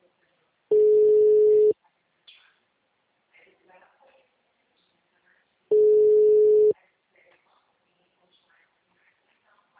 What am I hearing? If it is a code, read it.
Ringback tone (Europe, 425 Hz), with speech.
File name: out- -2010- - - .21